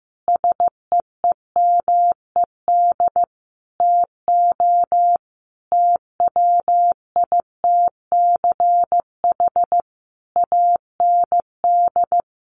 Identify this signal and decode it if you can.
Morse code, heard as a keyed tone, about 15 wpm, 700 Hz. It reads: SEEMED TO TWITCH AND
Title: A Sound Of Thunder 15wpm